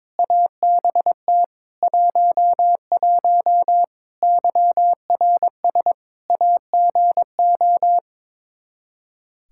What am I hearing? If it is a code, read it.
Morse code, heard as a keyed tone, about 21 wpm, 700 Hz. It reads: ABT 11 YRS AGO
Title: Practice QSO - Exchange 3 - More Info - 22wpm